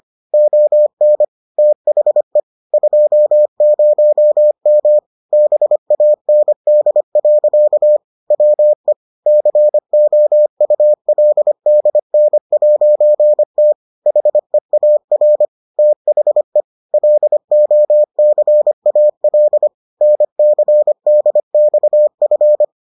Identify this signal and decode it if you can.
Morse code, heard as a keyed tone, about 25 wpm, 600 Hz. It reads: ON THE 20M BAND. WE COULDN'T HEAR THE LOCAL NCDXF